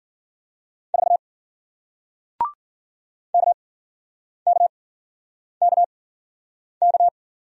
Morse code, heard as a keyed tone, about 53 wpm, spaced as for 9 wpm, 700 Hz. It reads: 4 KKKK